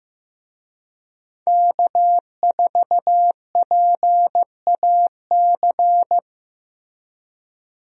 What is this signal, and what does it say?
Morse code, heard as a keyed tone, about 15 wpm, 700 Hz. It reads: K4PAC
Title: USA Call Signs - 5 Characters - Repeated 3x - 15wpm